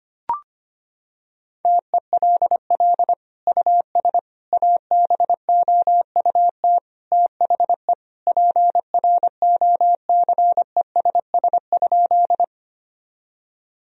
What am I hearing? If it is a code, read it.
Morse code, heard as a keyed tone, about 25 wpm, 700 Hz. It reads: TELL US ABOUT THE PROCESS?
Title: Sentences from Top 1000 Words - Review Part 3 - 25wpm